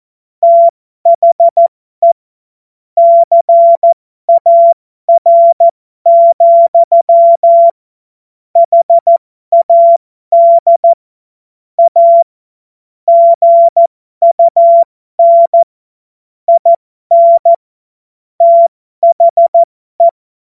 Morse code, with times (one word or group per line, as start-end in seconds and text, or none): 0.4-2.1 THE
3.0-7.7 CAR,
8.6-10.9 HAD
11.8-12.2 A
13.1-15.6 GUN
16.5-17.5 IN
18.4-20.1 THE